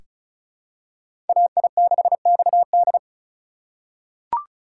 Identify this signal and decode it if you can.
Morse code, heard as a keyed tone, about 34 wpm, 700 Hz. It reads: AI6XD
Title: USA Call Signs - 5 Characters - 6x Character Spacing - 35wpm